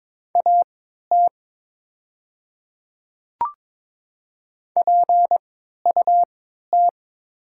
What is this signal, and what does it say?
Morse code, heard as a keyed tone, about 22 wpm, spaced as for 12 wpm, 700 Hz. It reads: AT PUT
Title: 3x Character Spacing - Top 300 Words - 22wpm